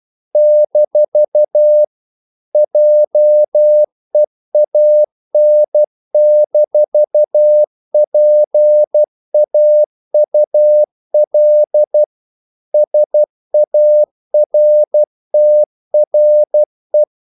Morse code, heard as a keyed tone, about 12 wpm, 600 Hz. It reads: - JEAN-PAUL SARTRE
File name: CW / Practice / EnglishQuotations0312